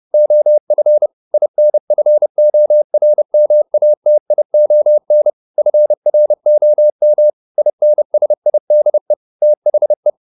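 Morse code, heard as a keyed tone, about 30 wpm, 600 Hz. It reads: OF INFORMATION FROM INSIDE THE
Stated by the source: Morse code at 30 wpm, the novel Never by Ken Follett in English, from CW med Gnister.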